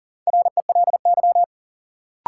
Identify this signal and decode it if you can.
Morse code, heard as a keyed tone, about 39 wpm, 700 Hz. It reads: RELY
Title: Sentences from Top 1200 Words - New Word Drill - 40wpm